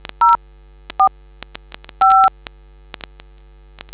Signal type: DTMF